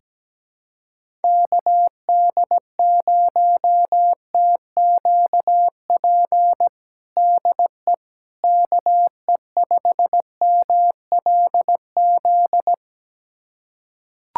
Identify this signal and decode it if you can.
Morse code, heard as a keyed tone, about 17 wpm, 700 Hz. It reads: KD0TQP DE KE5MLZ
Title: USA Callsign DE Callsign - 6 Characters - 17wpm